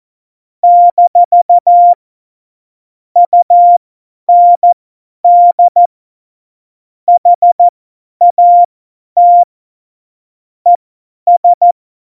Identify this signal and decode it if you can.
Morse code, heard as a keyed tone, about 14 wpm, spaced as for 10 wpm, 700 Hz. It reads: - UND HAT ES
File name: Morsekurs / Witze / Witz 26. Dezember 2017, 50 F70